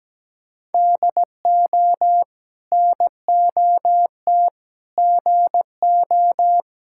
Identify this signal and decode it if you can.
Morse code, heard as a keyed tone, about 17 wpm, 700 Hz. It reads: DO NOT GO
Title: Sentences from Top 1000 Words - Review Part 6 - 17wpm